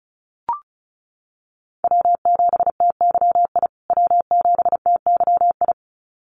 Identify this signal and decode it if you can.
Morse code, heard as a keyed tone, about 34 wpm, 700 Hz. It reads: W7TYI W7TYI
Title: USA Call Signs Repeated 2x 35wpm